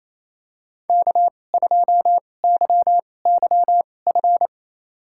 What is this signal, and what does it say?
Morse code, heard as a keyed tone, about 28 wpm, spaced as for 20 wpm, 700 Hz. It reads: K2YYF
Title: USA Call Signs - 5 Characters - 2x Character Spacing - 28wpm